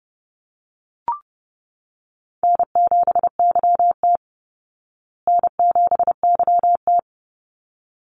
Morse code, heard as a keyed tone, about 30 wpm, 700 Hz. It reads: N7YT N7YT
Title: Easy Call Signs Repeated 2x - 30wpm